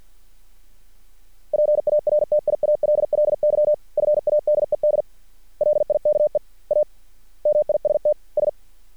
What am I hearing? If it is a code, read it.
Morse code, heard as a keyed tone, about 50 wpm, 600 Hz. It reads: PARTIALLY FADED LIKE A MIST S
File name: archived / CRSnet